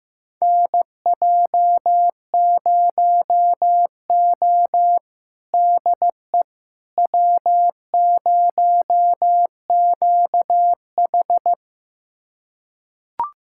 Morse code, heard as a keyed tone, about 15 wpm, 700 Hz. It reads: NJ0O DE W0QH
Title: USA Callsign DE Callsign - 4 Characters - 15wpm